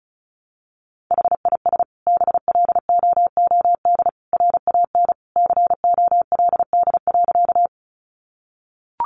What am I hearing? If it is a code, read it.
Morse code, heard as a keyed tone, about 34 wpm, 700 Hz. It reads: HIS BLOOD RAN COLD.
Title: Sentences from Top 1100 Words 35wpm